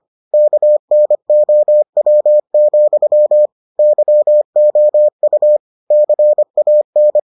Morse code, heard as a keyed tone, about 25 wpm, 600 Hz. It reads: KNOW, YOU CAN